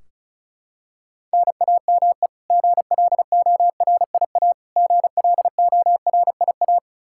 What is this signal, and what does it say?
Morse code, heard as a keyed tone, about 36 wpm, 700 Hz. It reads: NAME GLORIA GLORIA
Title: K1USN SST QSO Practice 35wpm